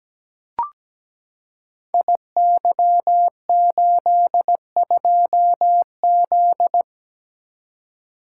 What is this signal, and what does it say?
Morse code, heard as a keyed tone, about 17 wpm, 700 Hz. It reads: IY82Z
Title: Code Groups - 5 Letter:Numbers - 17wpm